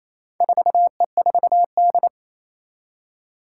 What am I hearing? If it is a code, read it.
Morse code, heard as a keyed tone, about 28 wpm, 700 Hz. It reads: 4E4D
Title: Four Challenging Letter-Number Combinations 28wpm